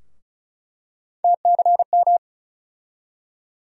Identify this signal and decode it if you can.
Morse code, heard as a keyed tone, about 34 wpm, 700 Hz. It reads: TCM
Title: USA Call Sign Postfixes - 3 Letters 35wpm